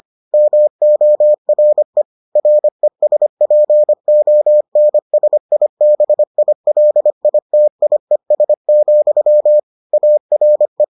Morse code, heard as a keyed tone, about 25 wpm, 600 Hz. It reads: MORE RESPONSIBILITIES, ARE